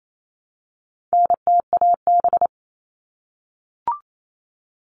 Morse code, heard as a keyed tone, about 28 wpm, 700 Hz. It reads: NTAB
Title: All Cut Numbers - 4-Digit 28wpm